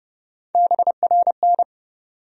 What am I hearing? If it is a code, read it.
Morse code, heard as a keyed tone, about 30 wpm, 700 Hz. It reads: BRN
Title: USA Call Sign Postfixes - 3 Letters 30wpm